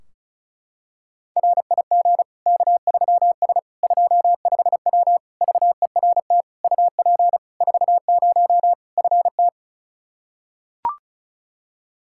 Morse code, heard as a keyed tone, about 34 wpm, 700 Hz. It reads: RIG K3S 25W VERT UP 40 FT
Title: K1USN SST QSO Practice 35wpm